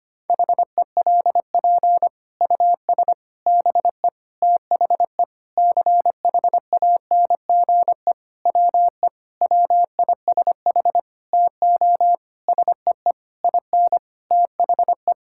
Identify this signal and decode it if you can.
Morse code, heard as a keyed tone, about 25 wpm, 700 Hz. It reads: HELP US BE THE CHANGE WE WISH TO SEE IN THE
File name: practice / 25wpm / Sentences with 9-Word Phrases - 25wpm